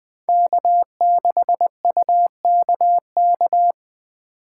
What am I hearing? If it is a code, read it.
Morse code, heard as a keyed tone, about 21 wpm, 700 Hz. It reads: K6UKK
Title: USA Call Signs - 5 Characters - 2x Character Spacing - 20wpm